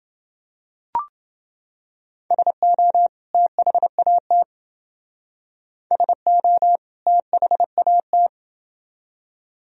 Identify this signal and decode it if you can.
Morse code, heard as a keyed tone, about 28 wpm, 700 Hz. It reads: SO THAT SO THAT